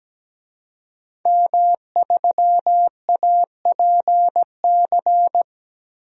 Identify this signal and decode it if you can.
Morse code, heard as a keyed tone, about 17 wpm, 700 Hz. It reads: M3APC
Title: DXCC-100 Call Signs - 5 Characters - Repeated 3x - 17wpm